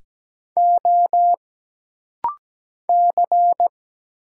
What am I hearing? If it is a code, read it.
Morse code, heard as a keyed tone, about 17 wpm, 700 Hz. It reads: O C
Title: Single Letter-Number Alt - Rapid-Fire 17wpm